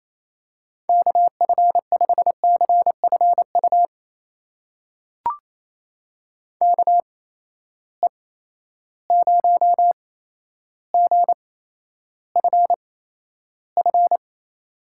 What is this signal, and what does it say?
Morse code, heard as a keyed tone, about 28 wpm, spaced as for 12 wpm, 700 Hz. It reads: KF5CFU KE0GFF
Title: USA Call Signs - 6 Characters - 8x Character Spacing - 28wpm